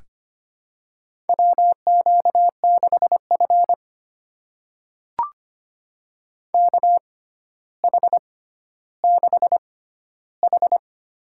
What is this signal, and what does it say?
Morse code, heard as a keyed tone, about 25 wpm, spaced as for 13 wpm, 700 Hz. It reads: WQ6F KH6H
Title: USA Call Signs - 4 Characters - 6x Character Spacing - 25wpm